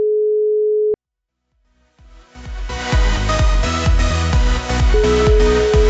Ringback tone (Europe, 425 Hz), with noise.